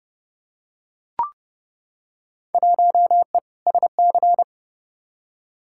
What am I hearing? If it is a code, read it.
Morse code, heard as a keyed tone, about 30 wpm, 700 Hz. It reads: 1E SC
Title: Field Day Exchanges - 30wpm